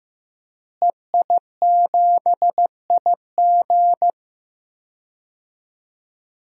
Morse code, heard as a keyed tone, about 15 wpm, 700 Hz. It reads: EI7IG
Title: DXCC-100 Call Signs - 5 Characters - Repeated 2x - 15wpm